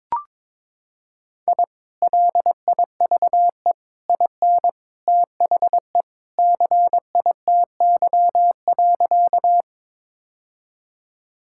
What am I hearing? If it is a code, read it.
Morse code, heard as a keyed tone, about 22 wpm, 700 Hz. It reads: I LIVE IN THE CITY.
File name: practice / 22wpm / Sentences with 2-Word Phrases - 22wpm